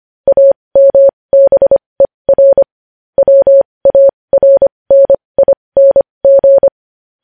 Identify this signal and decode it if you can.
Morse code, heard as a keyed tone, about 25 wpm, spaced as for 20 wpm, 550 Hz. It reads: AMBER WARNING